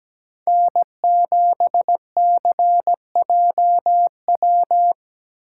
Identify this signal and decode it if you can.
Morse code, heard as a keyed tone, about 20 wpm, 700 Hz. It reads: N7CJW
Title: Speed Racing - USA 1x3 Call Signs 17wpm